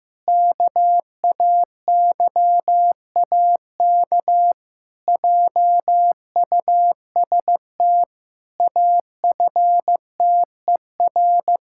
Morse code, heard as a keyed tone, about 16 wpm, 700 Hz. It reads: KAYAK JUST AFTER
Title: Pangram Challenge v2 - 15wpm